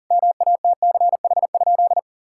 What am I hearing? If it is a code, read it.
Morse code, heard as a keyed tone, about 40 wpm, 700 Hz. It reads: MATCH?
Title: Sentences from Top 1600 Words 40wpm